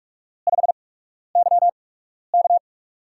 Morse code, heard as a keyed tone, about 44 wpm, spaced as for 12 wpm, 690 Hz. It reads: 5YK
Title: USA Call Signs - 4 Characters - 8x Character Spacing - 45wpm